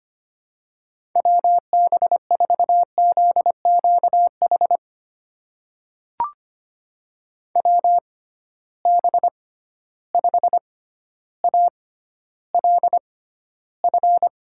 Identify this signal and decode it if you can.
Morse code, heard as a keyed tone, about 25 wpm, spaced as for 13 wpm, 700 Hz. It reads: WB4ZQH WB5ALF
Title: USA Call Signs - 6 Characters - 6x Character Spacing - 25wpm